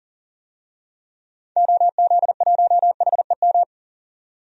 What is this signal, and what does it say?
Morse code, heard as a keyed tone, about 40 wpm, 700 Hz. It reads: OZ1HEM
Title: DXCC-100 Call Signs - 6 Characters - 40wpm